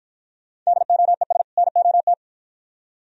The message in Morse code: DOES NOT

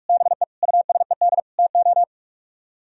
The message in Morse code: BE USED TO

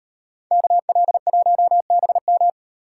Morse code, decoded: KL1BM

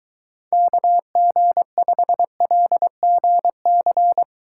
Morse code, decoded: KG5LGC

